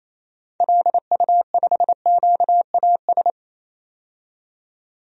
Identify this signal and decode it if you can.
Morse code, heard as a keyed tone, about 28 wpm, 700 Hz. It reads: LU5QAS
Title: DXCC-100 Call Signs - 6 Characters - 28wpm